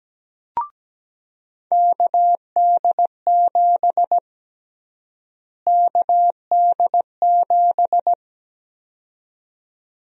Morse code, heard as a keyed tone, about 17 wpm, 700 Hz. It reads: KD7 KD7
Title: USA Call Sign Prefixes - Repeated 2x - 17wpm